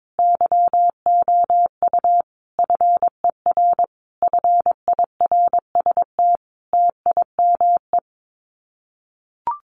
Morse code, heard as a keyed tone, about 22 wpm, 700 Hz. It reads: YOU FER FIRST TIME
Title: Common QSO Phrases 22wpm